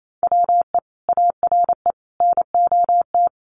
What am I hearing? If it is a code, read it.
Morse code, heard as a keyed tone, about 28 wpm, 700 Hz. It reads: WE ARE NOT
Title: Sentences from Top 1200 Words 28wpm